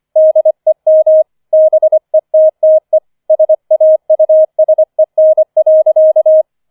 Morse code, heard as a keyed tone, about 25 wpm, 630 Hz. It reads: DEM BETTE SAUSEN.
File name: 005 Fünfter Streich 25wpm